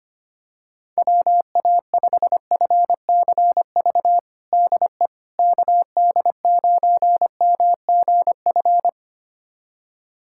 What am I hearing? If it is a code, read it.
Morse code, heard as a keyed tone, about 25 wpm, 700 Hz. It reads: WA5FCV DE KD9MGF
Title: USA Callsign DE Callsign - 6 Characters - 25wpm